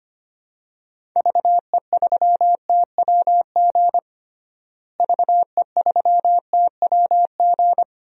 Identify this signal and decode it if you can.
Morse code, heard as a keyed tone, about 25 wpm, 700 Hz. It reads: VE3TWG VE3TWG
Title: DXCC-100 Call Signs - 6 Characters - Repeated 3x - 25wpm